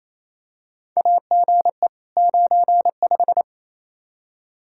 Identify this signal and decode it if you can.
Morse code, heard as a keyed tone, about 28 wpm, 700 Hz. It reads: AGE 95
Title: K1USN SST QSO Practice 28wpm